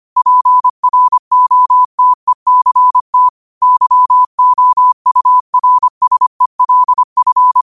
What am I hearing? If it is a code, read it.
Morse code, heard as a keyed tone, about 25 wpm, 1000 Hz. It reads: PROTECT YOURSELF